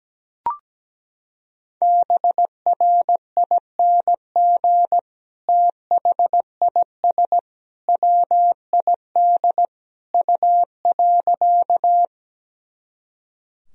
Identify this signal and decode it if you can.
Morse code, heard as a keyed tone, about 17 wpm, 700 Hz. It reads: BRING THIS WID U.
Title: Sentences from Top 500 Words and Abbreviations 17wpm